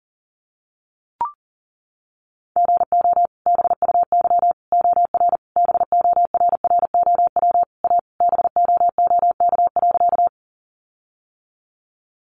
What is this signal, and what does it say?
Morse code, heard as a keyed tone, about 40 wpm, 700 Hz. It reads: GO BUY OR BORROW A BOOK.
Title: Sentences from Top 1500 Words 40wpm